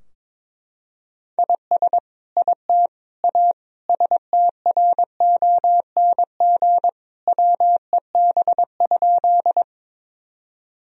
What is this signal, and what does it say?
Morse code, heard as a keyed tone, about 22 wpm, 700 Hz. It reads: IS IT A STRONG WEB?
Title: Sentences from Top 1000 Words - Review Part 4 - 22wpm